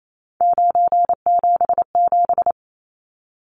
Morse code, heard as a keyed tone, about 28 wpm, 700 Hz. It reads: 977
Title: Numbers - 3-Digit 28wpm